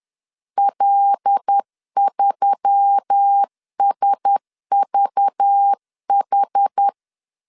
Morse code, heard as a keyed tone, about 10 wpm, 800 Hz. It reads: L3SVH
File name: LEZIONI / 7 GF9Z / 7 G 10wpm 2501